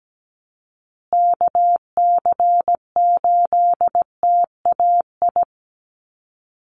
Morse code, heard as a keyed tone, about 17 wpm, 700 Hz. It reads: KC8TAI
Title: USA Call Signs Repeated 3x 17wpm